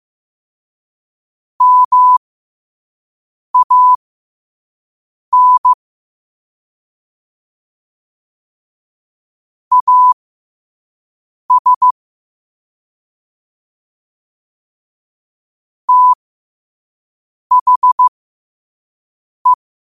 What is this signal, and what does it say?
Morse code, heard as a keyed tone, about 15 wpm, spaced as for 5 wpm, 1000 Hz. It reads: MAN AS THE
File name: qotdfiles / 5wpm